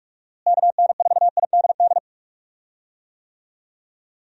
Morse code, heard as a keyed tone, about 45 wpm, 700 Hz. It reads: KN4IDD
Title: USA Call Signs - 6 Characters - Repeated 3x - 45wpm